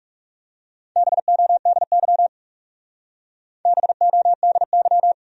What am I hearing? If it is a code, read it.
Morse code, heard as a keyed tone, about 36 wpm, 700 Hz. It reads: BODY BODY
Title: Speed-Racing Top 500 Words 30wpm